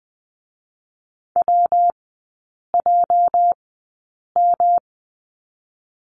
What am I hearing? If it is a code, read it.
Morse code, heard as a keyed tone, about 20 wpm, spaced as for 8 wpm, 690 Hz. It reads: WJM
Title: 205 Lesson 33e - Letter j - Three Characters 20wpm